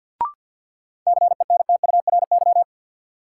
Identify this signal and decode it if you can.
Morse code, heard as a keyed tone, about 47 wpm, 700 Hz. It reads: CENTURY
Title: Sets of 5 Words - New Word Drill - 50wpm